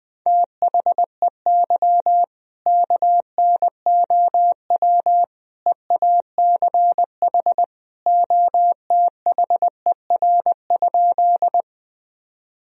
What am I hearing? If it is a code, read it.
Morse code, heard as a keyed tone, about 20 wpm, 700 Hz. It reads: THEY KNOW EACH OTHER?
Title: Sentences from Top 1000 Words - Review Part 1 - 20wpm